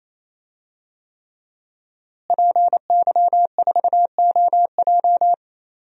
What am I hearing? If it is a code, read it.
Morse code, heard as a keyed tone, about 28 wpm, 700 Hz. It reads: PY4OJ
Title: DXCC-100 Call Signs - 5 Characters - Repeated 3x - 28wpm